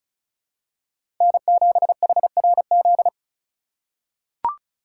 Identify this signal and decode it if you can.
Morse code, heard as a keyed tone, about 36 wpm, 700 Hz. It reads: N7HRZ
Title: USA Call Signs - 4x Character Spacing - 35wpm